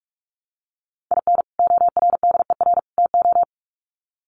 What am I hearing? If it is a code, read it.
Morse code, heard as a keyed tone, about 44 wpm, 700 Hz. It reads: IN ORDER TO
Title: Sets of 3 Words 45wpm